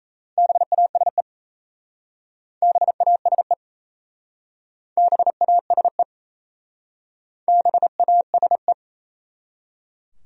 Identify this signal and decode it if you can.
Morse code, heard as a keyed tone, about 37 wpm, 700 Hz. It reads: BASE BASE BASE BASE